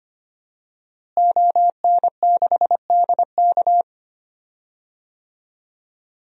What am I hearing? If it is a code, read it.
Morse code, heard as a keyed tone, about 25 wpm, 700 Hz. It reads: ON6DK
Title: DXCC-100 Call Signs - 5 Characters - Repeated 3x - 25wpm